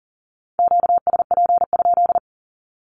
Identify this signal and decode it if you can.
Morse code, heard as a keyed tone, about 40 wpm, 700 Hz. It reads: QSP?
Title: Q-Codes 40wpm